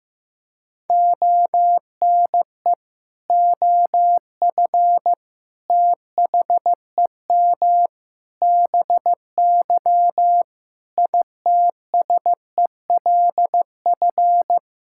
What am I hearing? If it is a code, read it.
Morse code, heard as a keyed tone, about 15 wpm, 700 Hz. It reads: ONE OF THEM BY ITSELF
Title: Sentences from Top 1000 Words - Review Part 5 - 15wpm